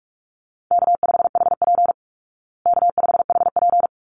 Morse code, heard as a keyed tone, about 44 wpm, 700 Hz. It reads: K5HP K5HP
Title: Call Signs Repeated 3x - 45wpm